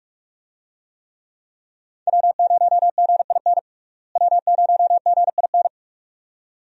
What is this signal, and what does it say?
Morse code, heard as a keyed tone, about 46 wpm, 700 Hz. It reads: W0GIN W0GIN
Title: USA Call Signs - 5 Characters - Repeated 2x - 45wpm